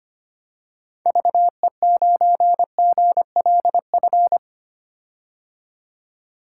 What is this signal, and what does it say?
Morse code, heard as a keyed tone, about 25 wpm, 700 Hz. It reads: VE9GLF
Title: K1USN SST QSO Practice - 25wpm